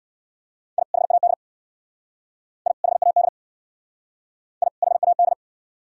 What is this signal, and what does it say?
Morse code, heard as a keyed tone, about 60 wpm, 700 Hz. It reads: I HAD I HAD I HAD
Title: Speed-Racing - Sets of 2 Words 50wpm